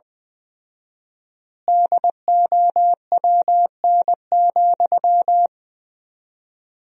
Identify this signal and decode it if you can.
Morse code, heard as a keyed tone, about 20 wpm, 700 Hz. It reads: DOWN,
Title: 4x Word-Spacing - Sentences from Top 500 Words - 20wpm